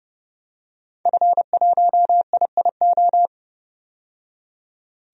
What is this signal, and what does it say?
Morse code, heard as a keyed tone, about 30 wpm, 700 Hz. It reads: F1IIO
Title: DXCC-100 Call Signs - 5 Characters - 30wpm